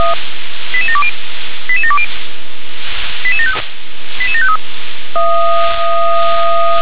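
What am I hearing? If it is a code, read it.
Five-tone selective call (ZVEI), heard as quick runs of five tones, tones of about 70 ms. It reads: 88720 88720 88753
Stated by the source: ZVEI five-tone call 88709 received by SDR.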